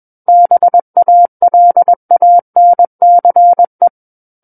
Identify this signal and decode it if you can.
Morse code, heard as a keyed tone, about 21 wpm, 700 Hz. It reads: BALANCE
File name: Book One / 12-Chapter 12 - 21 WPM